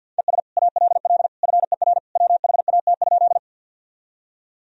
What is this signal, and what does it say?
Morse code, heard as a keyed tone, about 50 wpm, 700 Hz. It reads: ES ALL FER WHAT?
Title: Sentences from Top 100 Words and Abbreviations 50wpm